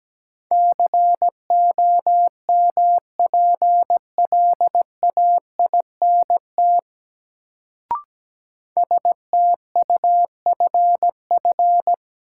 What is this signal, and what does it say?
Morse code, heard as a keyed tone, about 17 wpm, 700 Hz. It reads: COMPLAINT STUFF
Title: Sentences from Top 1400 Words - New Word Drill Alt - 17wpm